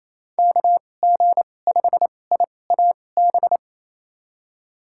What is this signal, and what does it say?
Morse code, heard as a keyed tone, about 28 wpm, spaced as for 20 wpm, 700 Hz. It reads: KG5IAB
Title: USA Call Signs - 6 Characters - 2x Character Spacing - 28wpm